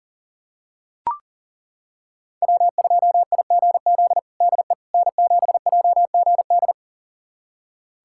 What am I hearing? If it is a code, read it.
Morse code, heard as a keyed tone, about 40 wpm, 700 Hz. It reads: W2IGZ DE N7JGD
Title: USA Callsign DE Callsign - 5 Characters - 40wpm